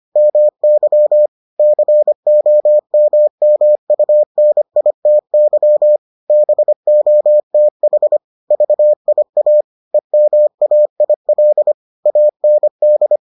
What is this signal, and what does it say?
Morse code, heard as a keyed tone, about 25 wpm, 600 Hz. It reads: MY COMMUNITY BOTH VIA EMAIL AND